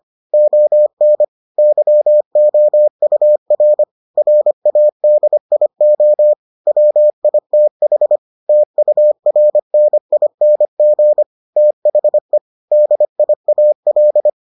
Morse code, heard as a keyed tone, about 25 wpm, 600 Hz. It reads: ON YOUR RADIO WITH TURNING THE DIAL